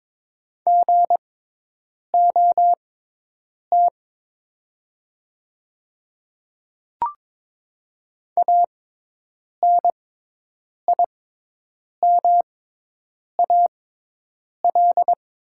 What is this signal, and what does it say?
Morse code, heard as a keyed tone, about 22 wpm, spaced as for 8 wpm, 700 Hz. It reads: GOT ANIMAL
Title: 6x Character Spacing - Top 300 Words - 22wpm